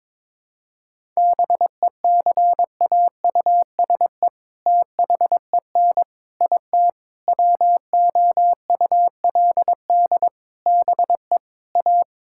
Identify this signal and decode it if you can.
Morse code, heard as a keyed tone, about 22 wpm, 700 Hz. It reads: BECAUSE THEN IT WOULD BE A
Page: morse-code-ninja-practice-22wpm